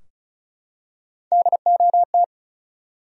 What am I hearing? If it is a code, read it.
Morse code, heard as a keyed tone, about 36 wpm, 700 Hz. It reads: DOT
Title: USA Common Names - Max 5 Letters - 35wpm